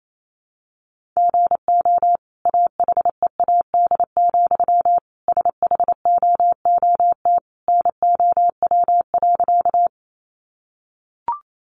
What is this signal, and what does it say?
Morse code, heard as a keyed tone, about 28 wpm, 700 Hz. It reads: GO AHEAD, SHOOT NOW.
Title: Sentences from Top 1200 Words 28wpm